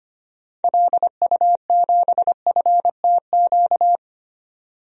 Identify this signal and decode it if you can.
Morse code, heard as a keyed tone, about 25 wpm, 700 Hz. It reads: LU7FTQ
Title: DXCC-100 Call Signs - 6 Characters - 25wpm